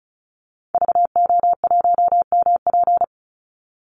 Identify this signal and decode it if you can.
Morse code, heard as a keyed tone, about 34 wpm, 700 Hz. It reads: VO1MP
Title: Call Signs Repeated 2x - 35wpm